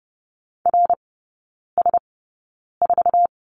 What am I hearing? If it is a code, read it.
Morse code, heard as a keyed tone, about 30 wpm, spaced as for 9 wpm, 700 Hz. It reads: RS4
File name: course / 30:10wpm / 087 Lesson 15e - Number 5 - Three Characters 30wpm